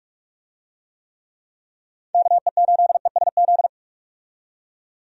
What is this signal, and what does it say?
Morse code, heard as a keyed tone, about 45 wpm, 700 Hz. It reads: KE8ESZ
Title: USA Call Signs - 6 Characters - 45wpm